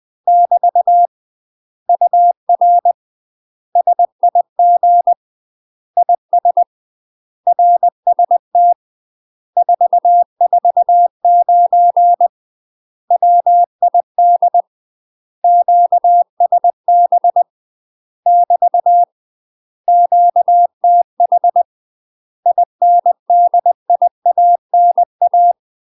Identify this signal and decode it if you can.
Morse code, heard as a keyed tone, about 20 wpm, 700 Hz. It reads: <BT> UR SIG IS RST 449 WID QSB <BT> QTH INDIANA
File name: UK MASTER QSOs TRAINING/20WPM